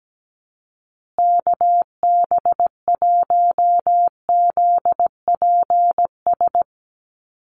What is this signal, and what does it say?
Morse code, heard as a keyed tone, about 17 wpm, 700 Hz. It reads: KB1ZPS
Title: USA Call Signs Repeated 2x 17wpm